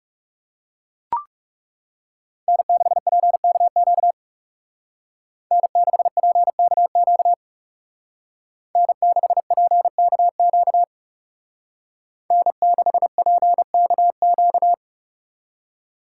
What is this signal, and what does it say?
Morse code, heard as a keyed tone, about 37 wpm, 700 Hz. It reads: N6PKQ N6PKQ N6PKQ N6PKQ
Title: Speed Racing - USA 1x3 Call Signs 30wpm